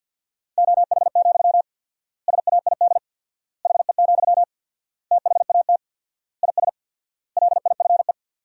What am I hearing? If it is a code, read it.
Morse code, heard as a keyed tone, about 50 wpm, 700 Hz. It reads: OH, SAID HE, THAT IS LIFE
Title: 4x Word-Spacing - Sentences from Top 500 Words - 50wpm